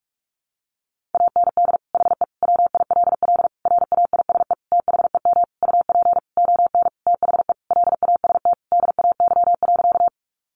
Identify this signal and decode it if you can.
Morse code, heard as a keyed tone, about 44 wpm, 700 Hz. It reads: AND HE WILL RAISE THEM UP ON THE LAST DAY.
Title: Sentences from Top 1400 Words 45wpm